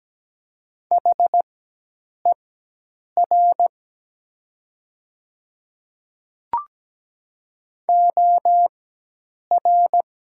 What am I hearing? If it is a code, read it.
Morse code, heard as a keyed tone, about 17 wpm, spaced as for 8 wpm, 700 Hz. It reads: HER OR
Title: 4x Character Spacing - Top 300 Words - 17wpm